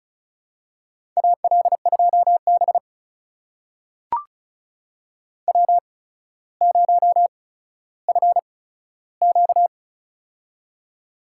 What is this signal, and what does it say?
Morse code, heard as a keyed tone, about 36 wpm, spaced as for 15 wpm, 700 Hz. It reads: AL2B W0FQ
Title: USA Call Signs - 8x Character Spacing - 35wpm